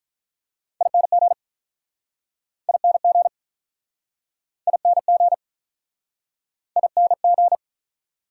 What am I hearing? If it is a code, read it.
Morse code, heard as a keyed tone, about 49 wpm, 700 Hz. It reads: ING ING ING ING